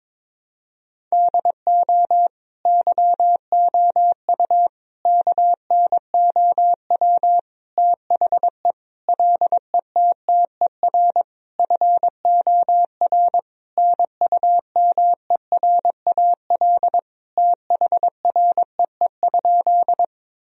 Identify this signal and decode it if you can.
Morse code, heard as a keyed tone, about 22 wpm, 700 Hz. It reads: DO YOU KNOW THE LETTER FOR NUMERAL THREE?
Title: Sentences from Top 1000 Words - Review Part 2 - 22wpm